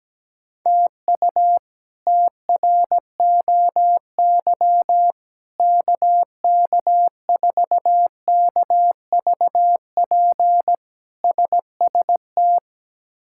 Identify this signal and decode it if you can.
Morse code, heard as a keyed tone, about 17 wpm, 700 Hz. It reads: TU TROY KK4KVP SST
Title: K1USN SST QSO Practice - 17wpm